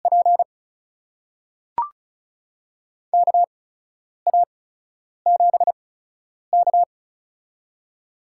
Morse code, heard as a keyed tone, about 36 wpm, spaced as for 10 wpm, 700 Hz. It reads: P KA7K
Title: USA Call Signs - 4 Characters - 8x Character Spacing - 35wpm